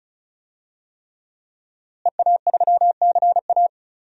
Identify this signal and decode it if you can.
Morse code, heard as a keyed tone, about 36 wpm, 700 Hz. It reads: EA3CA
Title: DXCC-100 Call Signs - 5 Characters - Repeated 3x - 35wpm